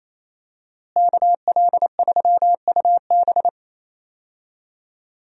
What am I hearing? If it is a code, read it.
Morse code, heard as a keyed tone, about 28 wpm, 700 Hz. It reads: KL3UB
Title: USA Call Signs - 5 Characters - Repeated 2x - 28wpm